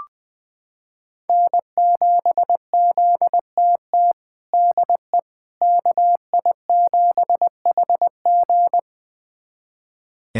Morse code, heard as a keyed tone, about 20 wpm, 700 Hz. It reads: N7ZTT DE KI7HG
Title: USA Callsign DE Callsign - 5 Characters - 20wpm